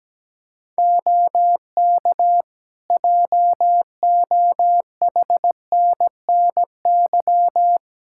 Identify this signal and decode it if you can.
Morse code, heard as a keyed tone, about 17 wpm, 700 Hz. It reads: OK JOHNNY
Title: Practice QSO - Exchange 3 - More Info - 17wpm